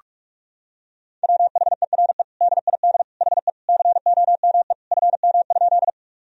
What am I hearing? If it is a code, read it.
Morse code, heard as a keyed tone, about 46 wpm, 700 Hz. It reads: WHERE DID HE COME FM?
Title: Sentences from Top 200 Words and Abbreviations 45wpm